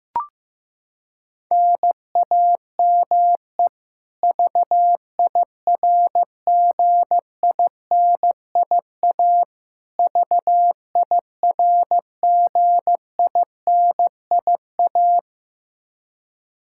Morse code, heard as a keyed tone, about 15 wpm, 700 Hz. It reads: NAME VIRGINIA VIRGINIA
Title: Practice QSO - Exchange 1 - Beginning - 15wpm